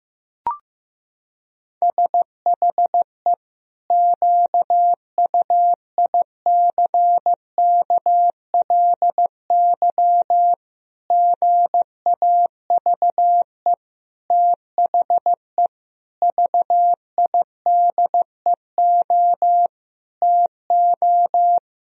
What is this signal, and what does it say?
Morse code, heard as a keyed tone, about 16 wpm, 700 Hz. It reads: SHE QUICKLY GAVE THE VIDEO TO
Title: Sentences from Top 1000 Words - Review Part 4 - 15wpm